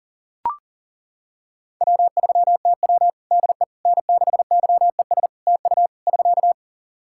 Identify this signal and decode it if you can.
Morse code, heard as a keyed tone, about 40 wpm, 700 Hz. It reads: W3TW DE N6YES TU <SK>
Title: Practice QSO - Exchange 4 - Ending - 40wpm